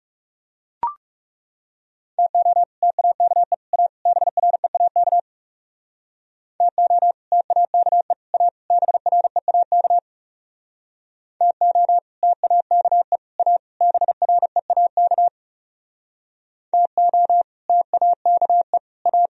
Morse code, heard as a keyed tone, about 37 wpm, 700 Hz. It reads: TO TAKE A BREAK TO TAKE A BREAK TO TAKE A BREAK TO TAKE A
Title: Speed-Racing - Sets of 4 Words 30wpm